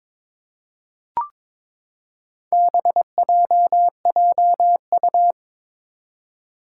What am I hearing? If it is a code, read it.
Morse code, heard as a keyed tone, about 22 wpm, 700 Hz. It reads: BJJU
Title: Four Challenging Letter-Number Combinations 22wpm